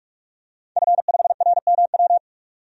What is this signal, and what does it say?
Morse code, heard as a keyed tone, about 44 wpm, 700 Hz. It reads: F5RMW
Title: DXCC-100 Call Signs - 5 Characters - 45wpm